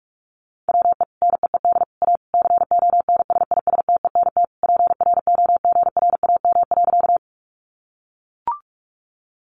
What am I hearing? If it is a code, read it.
Morse code, heard as a keyed tone, about 44 wpm, 700 Hz. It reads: WE NEED A CONSISTENT PROGRAM.
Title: Sentences from Top 1500 Words 45wpm